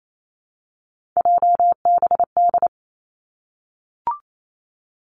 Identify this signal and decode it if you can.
Morse code, heard as a keyed tone, about 28 wpm, 700 Hz. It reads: JBD
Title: Three Challenging Letter-Number Combinations 28wpm